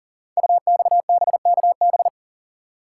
Morse code, heard as a keyed tone, about 42 wpm, 700 Hz. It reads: UXBKB
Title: Code Groups - 5 Letters - 40wpm